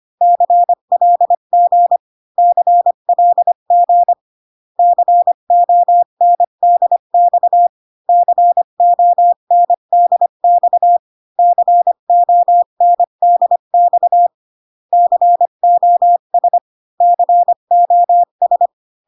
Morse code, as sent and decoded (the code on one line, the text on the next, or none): -.-. .-.. --. / -.-. .-.. --. / -.-. --- -. -.. -..- / -.-. --- -. -.. -..- / -.-. --- -. -.. -..- / -.-. --- ... / -.-. --- ...
CLG CLG CONDX CONDX CONDX COS COS